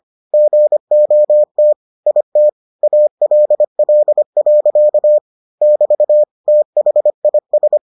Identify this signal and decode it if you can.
Morse code, heard as a keyed tone, about 25 wpm, 600 Hz. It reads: GOT IT ALL. <BT> THIS